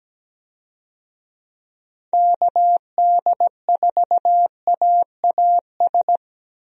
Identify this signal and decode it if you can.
Morse code, heard as a keyed tone, about 17 wpm, 700 Hz. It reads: KD4AAS
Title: USA Call Signs - 6 Characters - 17wpm